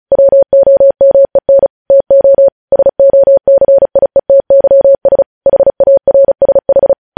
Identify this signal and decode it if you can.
Morse code, heard as a keyed tone, about 36 wpm, 550 Hz. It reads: WOMEN TO SOCIETYS HARSH
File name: podcast / News headlines-1x-35x35